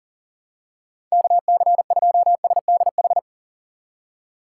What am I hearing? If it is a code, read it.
Morse code, heard as a keyed tone, about 40 wpm, 700 Hz. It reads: KC2SDH
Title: USA Call Signs - 6 Characters - 3x Character Spacing - 40wpm